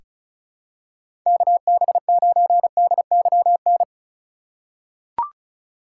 Morse code, heard as a keyed tone, about 36 wpm, 700 Hz. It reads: KB9DYN
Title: USA Call Signs - 4x Character Spacing - 35wpm